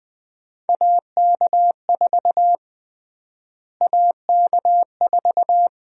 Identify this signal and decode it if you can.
Morse code, heard as a keyed tone, about 20 wpm, 700 Hz. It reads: AK4 AK4
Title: USA Call Sign Prefixes - Repeated 2x - 20wpm